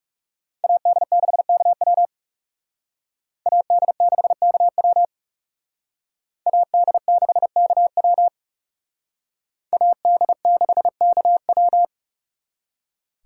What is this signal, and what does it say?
Morse code, heard as a keyed tone, about 36 wpm, 700 Hz. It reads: AD6KW AD6KW AD6KW AD6KW
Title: Speed Racing - USA 2x2 Call Signs 30wpm